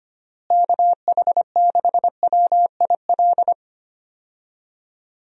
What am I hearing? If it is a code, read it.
Morse code, heard as a keyed tone, about 25 wpm, 700 Hz. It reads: KH6WIL
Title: USA Call Signs - 6 Characters - Repeated 3x - 25wpm